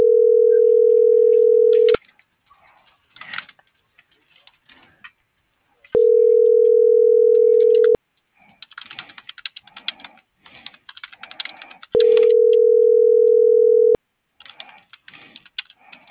Ringback tone (North America), 440 + 480 Hz.